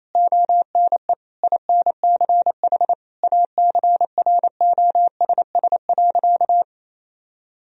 Morse code, heard as a keyed tone, about 28 wpm, 700 Hz. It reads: ONE INCH ACROSS.
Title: Sentences from Top 1000 Words - Review Part 3 - 28wpm